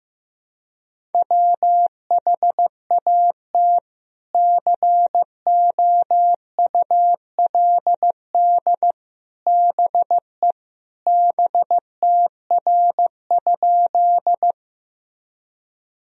Morse code, heard as a keyed tone, about 15 wpm, 700 Hz. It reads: WHAT COULD BE BTR?
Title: Sentences from Top 400 Words and Abbreviations 15wpm